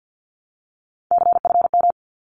Morse code, heard as a keyed tone, about 50 wpm, 700 Hz. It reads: CFM
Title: Abbreviations Repeated 3x - 50wpm